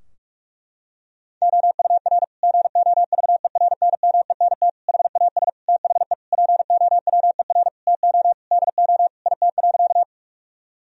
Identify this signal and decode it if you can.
Morse code, heard as a keyed tone, about 41 wpm, 700 Hz. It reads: OUR GOVERNMENT HAS THE POWER TO DO IT.
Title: Sentences from Top 1000 Words - Review Part 3 - 45wpm